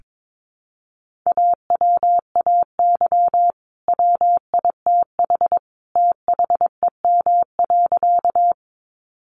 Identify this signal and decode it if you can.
Morse code, heard as a keyed tone, about 22 wpm, 700 Hz. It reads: AWAY WITH THEM.
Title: Sentences from Top 600 Words 22wpm